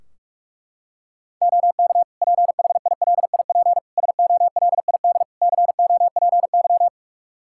Morse code, heard as a keyed tone, about 41 wpm, 700 Hz. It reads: OK PHILIP SOLID COPY
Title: K1USN SST QSO Practice 45wpm